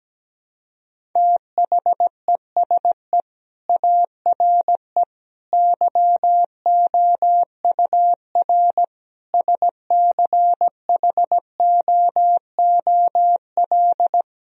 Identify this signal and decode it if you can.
Morse code, heard as a keyed tone, about 17 wpm, 700 Hz. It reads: THESE ARE YOUR SCHOOL
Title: Sentences from Top 1000 Words - Review Part 1 - 17wpm